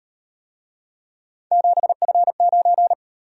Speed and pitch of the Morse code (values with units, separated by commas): 36 wpm, 700 Hz